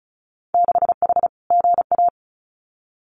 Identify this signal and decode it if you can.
Morse code, heard as a keyed tone, about 36 wpm, 700 Hz. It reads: 6H GA
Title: Winter Field Day Exchanges Encore 35wpm